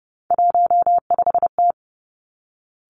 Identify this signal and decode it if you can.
Morse code, heard as a keyed tone, about 30 wpm, 700 Hz. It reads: 15T
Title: Three Challenging Letter-Number Combinations 30wpm